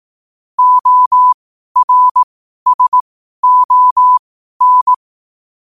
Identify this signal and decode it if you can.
Morse code, heard as a keyed tone, about 18 wpm, spaced as for 12 wpm, 1000 Hz. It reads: ORSON
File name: qotdfiles / 12wpm